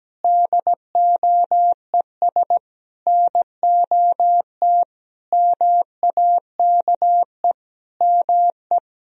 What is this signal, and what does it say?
Morse code, heard as a keyed tone, about 17 wpm, 700 Hz. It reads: DOES NOT MAKE ME
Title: Sentences from Top 1000 Words - Review Part 2 - 17wpm